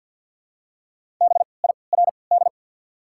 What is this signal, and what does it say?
Morse code, heard as a keyed tone, about 54 wpm, spaced as for 29 wpm, 690 Hz. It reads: BIRD
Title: 3x Character Spacing - Top 500 Words - 50wpm